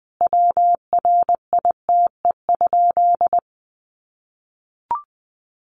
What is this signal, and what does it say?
Morse code, heard as a keyed tone, about 20 wpm, 700 Hz. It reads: WRITE?
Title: Sentences from Top 600 Words 20wpm